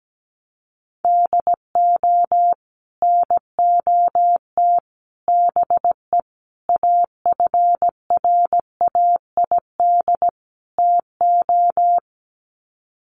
Morse code, heard as a keyed tone, about 17 wpm, 700 Hz. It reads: DO NOT BE AFRAID TO
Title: Sets of 5 Words Encore2 17wpm